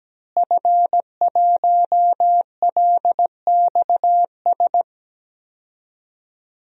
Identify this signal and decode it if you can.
Morse code, heard as a keyed tone, about 17 wpm, 700 Hz. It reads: F1LXS
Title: DXCC-100 Call Signs - 5 Characters - Repeated 3x - 17wpm